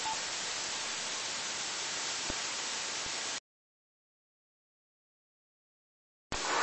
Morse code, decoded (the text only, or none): none